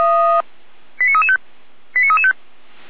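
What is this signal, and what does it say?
Five-tone selective call (ZVEI), heard as quick runs of five tones, tones of about 70 ms. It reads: 89335 89335
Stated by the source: Five-tone ZVEI selective call received on SDR.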